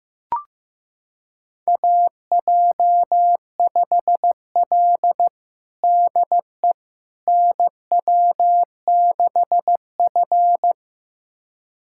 Morse code, heard as a keyed tone, about 16 wpm, 700 Hz. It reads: AJ5L DE NW6F
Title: USA Callsign DE Callsign - 4 Characters - 15wpm